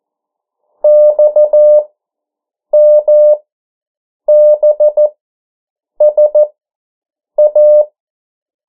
Morse code, heard as a keyed tone, about 14 wpm, spaced as for 7 wpm, 600 Hz. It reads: XMBSA
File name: morse mp3 / 09 urok 8